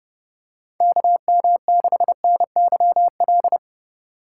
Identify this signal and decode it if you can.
Morse code, heard as a keyed tone, about 31 wpm, 700 Hz. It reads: KM6NYL
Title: USA Call Signs - 6 Characters - 3x Character Spacing - 30wpm